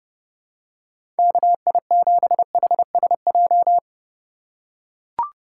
Morse code, heard as a keyed tone, about 32 wpm, 700 Hz. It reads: KI7HSJ
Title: USA Call Signs - 6 Characters - 3x Character Spacing - 30wpm